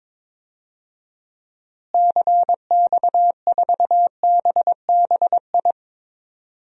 Morse code, heard as a keyed tone, about 22 wpm, 700 Hz. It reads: CX4BBI
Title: DXCC-100 Call Signs - 6 Characters - Repeated 2x - 22wpm